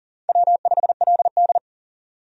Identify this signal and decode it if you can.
Morse code, heard as a keyed tone, about 37 wpm, 700 Hz. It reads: W5LD
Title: USA Call Signs - 4 Characters - 40wpm